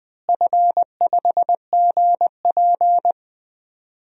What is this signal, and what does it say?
Morse code, heard as a keyed tone, about 20 wpm, 700 Hz. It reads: F5GP
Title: DXCC-100 Call Signs - 4 Characters - Repeated 2x - 20wpm